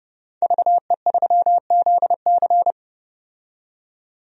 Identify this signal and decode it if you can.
Morse code, heard as a keyed tone, about 30 wpm, 700 Hz. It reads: VE3ZC
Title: DXCC-100 Call Signs - 5 Characters - 30wpm